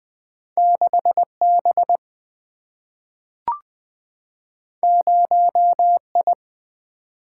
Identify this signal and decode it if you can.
Morse code, heard as a keyed tone, about 20 wpm, 700 Hz. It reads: 6 B 0 I
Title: Single-Letter Single-Number 20wpm